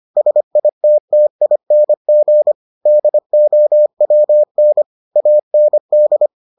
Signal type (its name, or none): Morse code